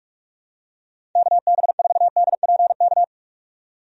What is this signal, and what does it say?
Morse code, heard as a keyed tone, about 46 wpm, 700 Hz. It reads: KB4DPK
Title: USA Call Signs - 6 Characters - 45wpm